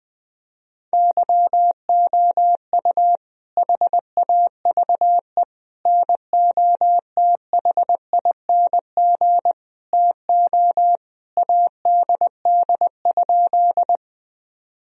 Morse code, heard as a keyed tone, about 19 wpm, 700 Hz. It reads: YOU HAVE NOTHING TO ADD?
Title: Sentences from Top 1000 Words - Review Part 1 - 20wpm